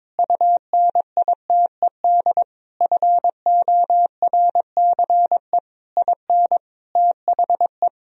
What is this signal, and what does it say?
Morse code, heard as a keyed tone, about 22 wpm, 700 Hz. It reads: UNITED FORCE IN THE
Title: Sentences from Top 1000 Words - Review Part 4 - 22wpm